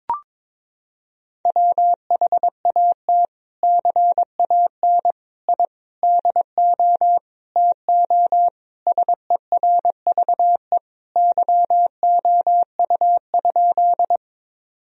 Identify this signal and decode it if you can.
Morse code, heard as a keyed tone, about 22 wpm, 700 Hz. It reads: WHAT CAN I DO TO SERVE YOU?
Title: Sentences from Top 1000 Words - Review Part 2 - 22wpm